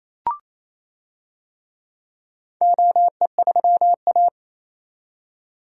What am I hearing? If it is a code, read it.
Morse code, heard as a keyed tone, about 28 wpm, 700 Hz. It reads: OE3A
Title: DXCC-100 Call Signs - 4 Characters - 28wpm